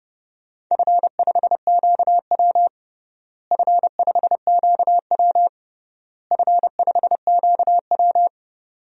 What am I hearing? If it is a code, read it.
Morse code, heard as a keyed tone, about 29 wpm, 700 Hz. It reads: F5QW F5QW F5QW
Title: DXCC-100 Call Signs - 4 Characters - Repeated 3x - 30wpm